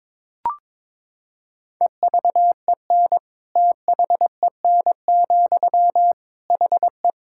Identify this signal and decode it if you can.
Morse code, heard as a keyed tone, about 21 wpm, 700 Hz. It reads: EVEN THEN, HE
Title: Sentences from Top 1600 Words 22wpm